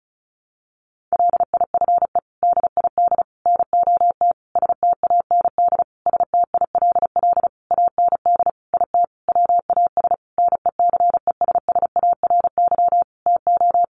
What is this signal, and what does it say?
Morse code, heard as a keyed tone, about 34 wpm, 700 Hz. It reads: LIFE DID NOT STAND STILL AND IT WAS NECESSARY TO
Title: Sentences from Top 800 Words 35wpm